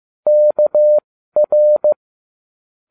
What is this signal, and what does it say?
Morse code, heard as a keyed tone, about 15 wpm, spaced as for 12 wpm, 600 Hz. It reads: KR